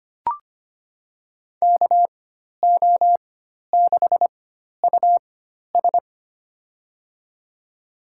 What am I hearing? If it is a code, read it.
Morse code, heard as a keyed tone, about 25 wpm, spaced as for 12 wpm, 700 Hz. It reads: KO6US
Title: USA Call Signs - 4x Character Spacing - 25wpm